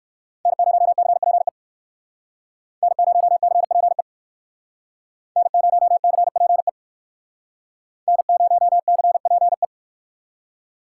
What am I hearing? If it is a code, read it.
Morse code, heard as a keyed tone, about 58 wpm, 700 Hz. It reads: N0CPE N0CPE N0CPE N0CPE